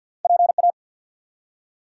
Morse code, heard as a keyed tone, about 50 wpm, 700 Hz. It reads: PA
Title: US States and Canadian Provinces 50wpm